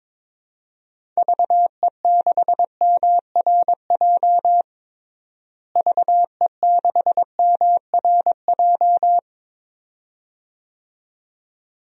Morse code, heard as a keyed tone, about 22 wpm, 700 Hz. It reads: VE6MRJ VE6MRJ
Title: DXCC-100 Call Signs - 6 Characters - Repeated 2x - 22wpm